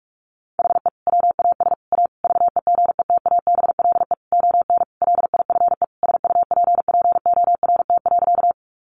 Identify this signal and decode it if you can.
Morse code, heard as a keyed tone, about 44 wpm, 700 Hz. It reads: HE WAS A VEGETABLE ON LIFE SUPPORT.
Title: Sentences from Top 1300 Words 45wpm